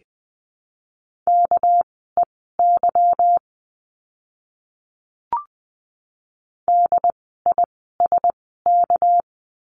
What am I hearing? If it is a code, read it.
Morse code, heard as a keyed tone, about 20 wpm, spaced as for 14 wpm, 700 Hz. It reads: KEY DISK
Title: 199 Lesson 32f - Letter k - Words 20wpm